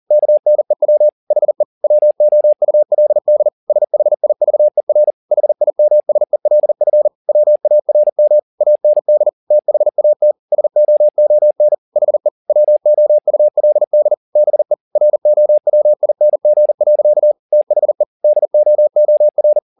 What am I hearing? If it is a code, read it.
Morse code, heard as a keyed tone, about 40 wpm, 600 Hz. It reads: KNEW HE WOULD SHIVER HIMSELF WARM AND THAT SOON HE WOULD BE ROWING. THE DOOR